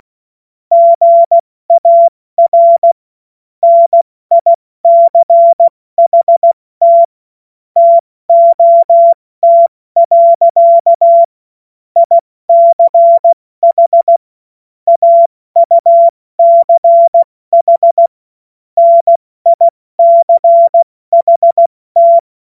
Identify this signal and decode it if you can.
Morse code, heard as a keyed tone, about 16 wpm, spaced as for 14 wpm, 680 Hz. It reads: GAR NICHT TOT. ICH AUCH NICHT
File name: Morsekurs / Witze / Witz 04. Januar 2022, 70 F80